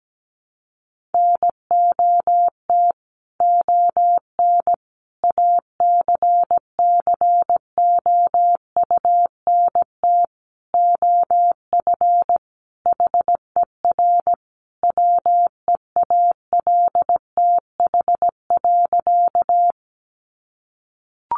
Morse code, heard as a keyed tone, about 17 wpm, 700 Hz. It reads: NOT ON ACCOUNT OF HER WEALTH.
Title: Sentences from Top 1200 Words 17wpm